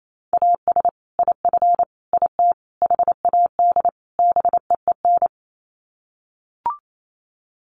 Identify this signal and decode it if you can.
Morse code, heard as a keyed tone, about 28 wpm, 700 Hz. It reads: AS IF IT HAD BEEN
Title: Sets of 5 Words Encore2 28wpm